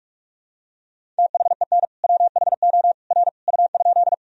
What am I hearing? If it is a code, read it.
Morse code, heard as a keyed tone, about 46 wpm, 700 Hz. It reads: THEN WHO R U?
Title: Sentences from Top 100 Words and Abbreviations 45wpm